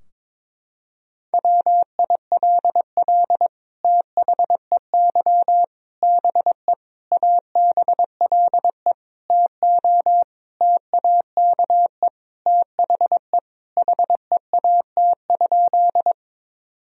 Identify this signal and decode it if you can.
Morse code, heard as a keyed tone, about 22 wpm, 700 Hz. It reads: WILL THEY BE ABLE TO TAKE THE HEAT?
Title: Sentences from Top 1000 Words - Review Part 3 - 22wpm